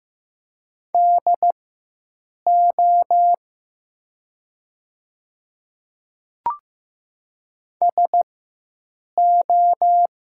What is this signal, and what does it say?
Morse code, heard as a keyed tone, about 15 wpm, spaced as for 7 wpm, 700 Hz. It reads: DO SO
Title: 4x Character Spacing - Top 100 Words - 15wpm